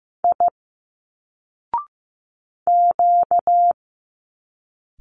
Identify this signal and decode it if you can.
Morse code, heard as a keyed tone, about 15 wpm, 700 Hz. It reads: I Q
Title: Single Letters 15wpm